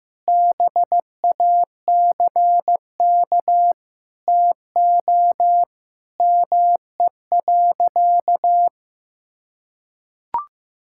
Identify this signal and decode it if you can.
Morse code, heard as a keyed tone, about 15 wpm, 700 Hz. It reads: BACK TO ME.
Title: Sentences from Top 1000 Words - Review Part 6 - 15wpm